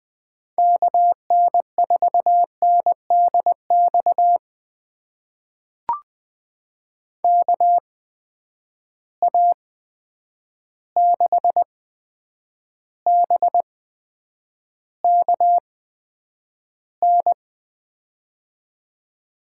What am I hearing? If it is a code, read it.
Morse code, heard as a keyed tone, about 20 wpm, spaced as for 9 wpm, 700 Hz. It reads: KN4NDX KA6BKN
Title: USA Call Signs - 8x Character Spacing - 20wpm